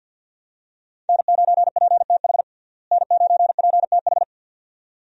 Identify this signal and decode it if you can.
Morse code, heard as a keyed tone, about 50 wpm, 700 Hz. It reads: N9PTH N9PTH
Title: USA Call Signs - 5 Characters - Repeated 2x - 50wpm